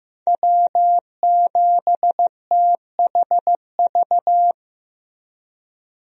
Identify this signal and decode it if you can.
Morse code, heard as a keyed tone, about 16 wpm, 700 Hz. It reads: W7THV
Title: USA Call Signs - 5 Characters - Repeated 2x - 15wpm